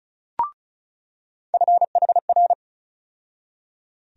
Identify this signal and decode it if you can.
Morse code, heard as a keyed tone, about 36 wpm, 700 Hz. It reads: FHR
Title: USA Call Sign Postfixes - 3 Letters 35wpm